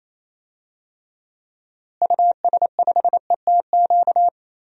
Morse code, heard as a keyed tone, about 28 wpm, 700 Hz. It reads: US5ETQ
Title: DXCC-100 Call Signs - 6 Characters - Repeated 3x - 28wpm